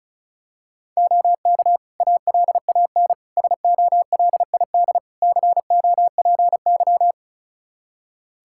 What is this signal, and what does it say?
Morse code, heard as a keyed tone, about 36 wpm, 700 Hz. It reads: OK ALAN SOLID COPY
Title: Practice QSO - Exchange 3 - More Info - 35wpm